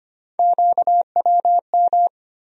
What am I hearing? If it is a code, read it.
Morse code, heard as a keyed tone, about 25 wpm, 700 Hz. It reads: QWM
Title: USA Call Sign Postfixes - 3 Letters 25wpm